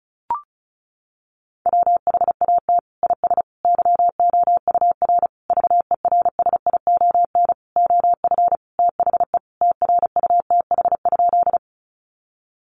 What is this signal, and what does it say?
Morse code, heard as a keyed tone, about 34 wpm, 700 Hz. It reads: WHAT IS YOUR VERSION OF THE TRUTH?
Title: Sentences from Top 1400 Words 35wpm